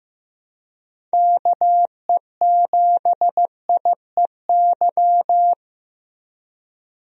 Morse code, heard as a keyed tone, about 15 wpm, 700 Hz. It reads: KE7IEY
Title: USA Call Signs - 6 Characters - Repeated 3x - 15wpm